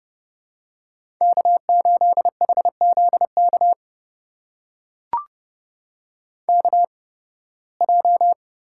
Morse code, heard as a keyed tone, about 30 wpm, spaced as for 20 wpm, 700 Hz. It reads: K8HZK KJ
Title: USA Call Signs - 8x Character Spacing - 30wpm